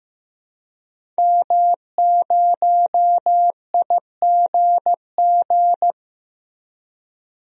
Morse code, heard as a keyed tone, about 16 wpm, 700 Hz. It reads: M0IGG
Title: DXCC-100 Call Signs - 5 Characters - Repeated 2x - 15wpm